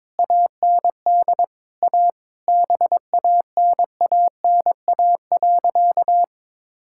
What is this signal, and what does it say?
Morse code, heard as a keyed tone, about 22 wpm, 700 Hz. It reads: AND A BANANA.
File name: practice / 22wpm / Sentences with 2-Word Phrases - 22wpm